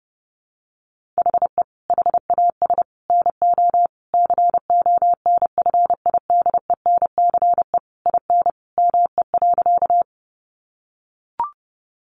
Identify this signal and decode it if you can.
Morse code, heard as a keyed tone, about 30 wpm, 700 Hz. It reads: HE HAS NO CONFIDENCE IN ME.
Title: Sentences from Top 1500 Words 30wpm